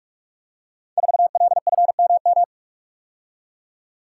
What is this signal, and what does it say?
Morse code, heard as a keyed tone, about 44 wpm, 700 Hz. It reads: 4LFMM
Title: Code Groups - 5 Letter:Numbers - 45wpm